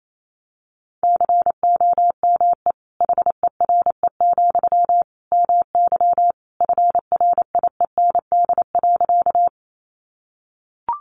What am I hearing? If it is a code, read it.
Morse code, heard as a keyed tone, about 28 wpm, 700 Hz. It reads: COME HERE, MY FRIEND.
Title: Sentences from Top 300 Words 28wpm